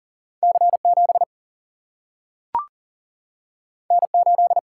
Morse code, heard as a keyed tone, about 40 wpm, 700 Hz. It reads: C 7 N 8
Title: Single-Letter Single-Number 40wpm